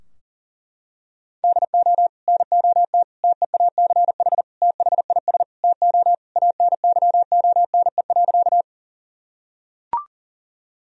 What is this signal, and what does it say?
Morse code, heard as a keyed tone, about 40 wpm, 700 Hz. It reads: DO NOT TEACH THIS TO ANYONE.
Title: Sentences from Top 1000 Words - Review Part 2 - 40wpm